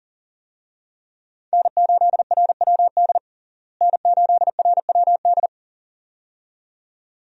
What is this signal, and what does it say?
Morse code, heard as a keyed tone, about 40 wpm, 700 Hz. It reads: N8RWD N8RWD
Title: USA Call Signs - 5 Characters - Repeated 2x - 40wpm